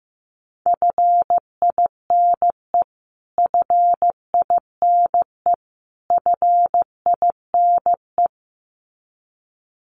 Morse code, heard as a keyed tone, about 15 wpm, 700 Hz. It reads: FINE FINE FINE
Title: Top 500 Words Repeated 15wpm